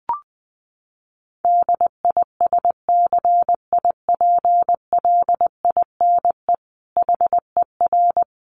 Morse code, heard as a keyed tone, about 20 wpm, 700 Hz. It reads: DISCIPLINE HER
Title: Sentences from Top 900 Words 20wpm